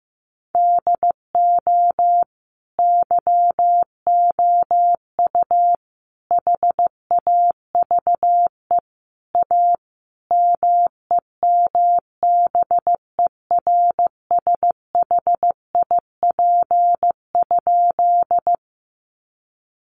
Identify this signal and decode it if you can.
Morse code, heard as a keyed tone, about 15 wpm, 700 Hz. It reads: DO YOU HAVE A MEMBERSHIP?
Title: Sentences from Top 1400 Words 15wpm